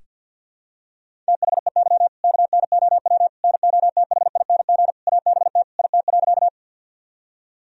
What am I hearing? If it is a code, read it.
Morse code, heard as a keyed tone, about 46 wpm, 700 Hz. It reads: THEY KNOW NOTHING ABT IT.
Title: Sentences from Top 500 Words and Abbreviations 50wpm